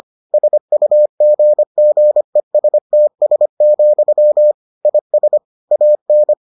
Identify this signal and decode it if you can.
Morse code, heard as a keyed tone, about 25 wpm, 600 Hz. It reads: SUGGESTS, IS AN